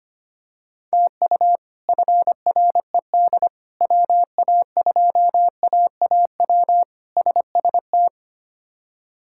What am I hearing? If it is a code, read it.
Morse code, heard as a keyed tone, about 25 wpm, 700 Hz. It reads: TU FRED WA2AAW SST
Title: K1USN SST QSO Practice - 25wpm